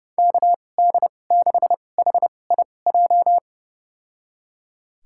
Morse code, heard as a keyed tone, about 32 wpm, spaced as for 22 wpm, 700 Hz. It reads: KD6HIJ
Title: USA Call Signs - 6 Characters - 2x Character Spacing - 30wpm